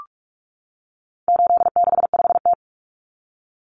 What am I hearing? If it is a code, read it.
Morse code, heard as a keyed tone, about 44 wpm, 700 Hz. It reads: 865T
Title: Common Cut Numbers - 4-Digit 45wpm